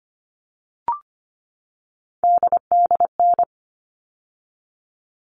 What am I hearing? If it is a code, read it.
Morse code, heard as a keyed tone, about 25 wpm, 700 Hz. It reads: DDN
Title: All Cut Numbers - 3-Digit 25wpm